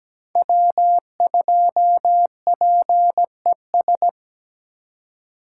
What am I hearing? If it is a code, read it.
Morse code, heard as a keyed tone, about 20 wpm, 700 Hz. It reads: W2PES
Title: Speed Racing - USA 1x3 Call Signs 17wpm